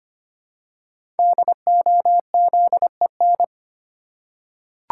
Morse code, heard as a keyed tone, about 25 wpm, 700 Hz. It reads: DOZEN
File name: 5 Letter Words - First 3 Letters Given - 25wpm